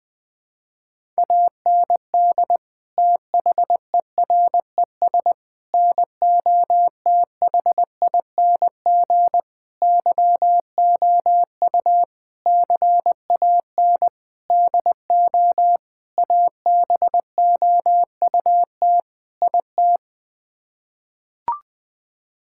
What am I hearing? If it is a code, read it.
Morse code, heard as a keyed tone, about 19 wpm, 700 Hz. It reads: AND THERES NOTHING YOU CAN DO ABOUT IT
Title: Sets of 8 Words - 20wpm